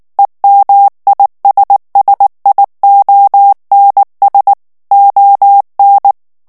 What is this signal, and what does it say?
Morse code, heard as a keyed tone, about 19 wpm, 800 Hz. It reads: EMISSIONS ON
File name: ANGLAIS texte clair 3 vitesse 18 mots minute